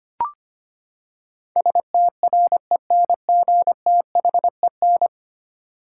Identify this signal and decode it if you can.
Morse code, heard as a keyed tone, about 25 wpm, 700 Hz. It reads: STRENGTHEN
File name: Ten Letter Words - 25wpm